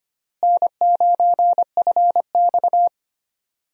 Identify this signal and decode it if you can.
Morse code, heard as a keyed tone, about 25 wpm, 700 Hz. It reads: N9FX
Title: USA Call Signs - 4 Characters - Repeated 2x - 25wpm